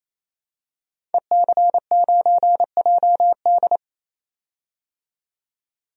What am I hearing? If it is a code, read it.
Morse code, heard as a keyed tone, about 28 wpm, 700 Hz. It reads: EC9JD
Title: DXCC-100 Call Signs - 5 Characters - 28wpm